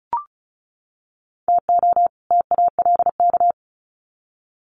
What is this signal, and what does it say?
Morse code, heard as a keyed tone, about 34 wpm, 700 Hz. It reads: TO TALK
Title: Sets of 2 Words 35wpm